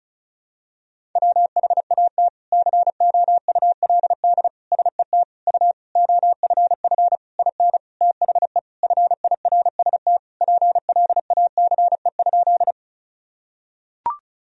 Morse code, heard as a keyed tone, about 36 wpm, 700 Hz. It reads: WHAT COULD SET U OFF IN THE FIRST PLACE?
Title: Sentences from Top 200 Words and Abbreviations 35wpm